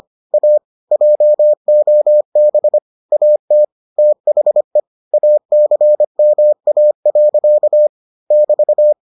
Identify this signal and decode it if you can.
Morse code, heard as a keyed tone, about 25 wpm, 600 Hz. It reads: A JOB AT THE ACMA. <BT>